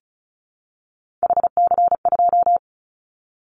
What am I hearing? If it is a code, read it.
Morse code, heard as a keyed tone, about 34 wpm, 700 Hz. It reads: HC2
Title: Callsign Prefixes for Top 100 DXCC - 35wpm